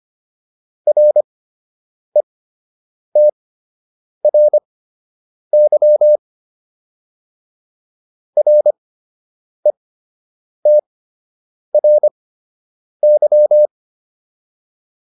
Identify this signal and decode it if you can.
Morse code, heard as a keyed tone, about 25 wpm, spaced as for 8 wpm, 600 Hz. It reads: RETRY RETRY